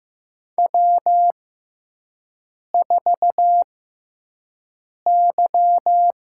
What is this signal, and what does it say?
Morse code, heard as a keyed tone, about 15 wpm, spaced as for 5 wpm, 700 Hz. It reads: W4Y
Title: USA Call Signs - 5 Characters - 6x Character Spacing - 15wpm